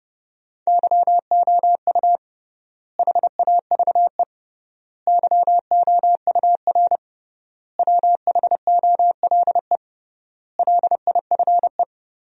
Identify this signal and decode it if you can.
Morse code, heard as a keyed tone, about 32 wpm, 700 Hz. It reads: YOU HAVE YOUR WHOLE LIFE
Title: 3x Word-Spacing - Sentences from Top 400 Words - 30wpm